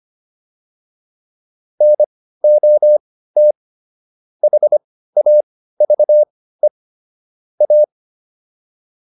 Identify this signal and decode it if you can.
Morse code, heard as a keyed tone, about 25 wpm, spaced as for 15 wpm, 600 Hz. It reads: NOT HAVE A
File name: WD205 15